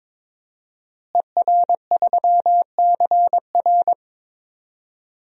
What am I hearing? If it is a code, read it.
Morse code, heard as a keyed tone, about 22 wpm, 700 Hz. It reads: ER3CR
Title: DXCC-100 Call Signs - 5 Characters - Repeated 2x - 22wpm